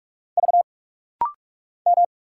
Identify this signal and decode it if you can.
Morse code, heard as a keyed tone, about 46 wpm, 690 Hz. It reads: V M
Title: Single Letters - Rapid-Fire 45wpm